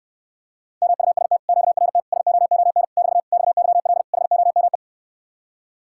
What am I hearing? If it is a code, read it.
Morse code, heard as a keyed tone, about 69 wpm, 700 Hz. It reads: MUST QRT SOON <BT> XYL HOME